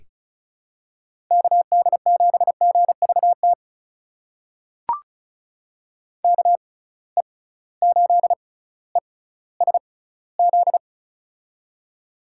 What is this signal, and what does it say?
Morse code, heard as a keyed tone, about 34 wpm, spaced as for 18 wpm, 700 Hz. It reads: KD7GVT KE8ESZ
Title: USA Call Signs - 6 Characters - 6x Character Spacing - 35wpm